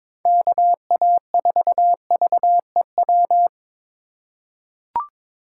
Morse code, heard as a keyed tone, about 22 wpm, 700 Hz. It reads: KA4VEW
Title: USA Call Signs - 6 Characters - 6x Character Spacing - 22wpm